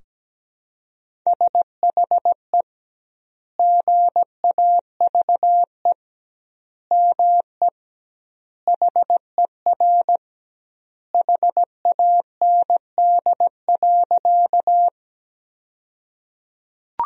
Morse code, heard as a keyed tone, about 17 wpm, 700 Hz. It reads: SHE GAVE ME HER HAND.
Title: 2x Word-Spacing - Sentences from Top 500 Words - 17wpm